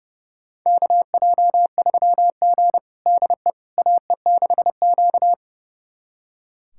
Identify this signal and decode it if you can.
Morse code, heard as a keyed tone, about 32 wpm, 700 Hz. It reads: KJ3G DE AE6Q
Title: USA Callsign DE Callsign - 4 Characters - 30wpm